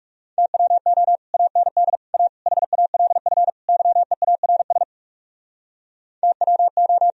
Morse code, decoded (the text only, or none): TWO AND A HALF YEARS TWO